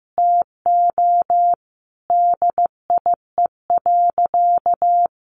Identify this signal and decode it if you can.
Morse code, heard as a keyed tone, about 15 wpm, 700 Hz. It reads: TO DIE.
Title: Sentences from Top 1400 Words 15wpm